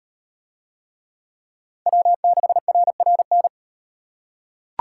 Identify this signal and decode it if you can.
Morse code, heard as a keyed tone, about 38 wpm, 700 Hz. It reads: W6RRN